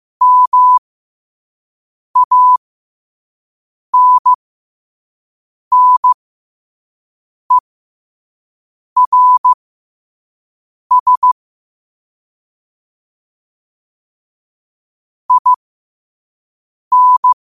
Morse code, heard as a keyed tone, about 15 wpm, spaced as for 5 wpm, 1000 Hz. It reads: MANNERS IN
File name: qotdfiles / 5wpm